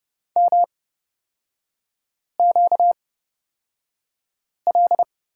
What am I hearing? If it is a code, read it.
Morse code, heard as a keyed tone, about 30 wpm, spaced as for 5 wpm, 700 Hz. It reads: MQL
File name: Lesson 38e - Letter q - Three Characters 30wpm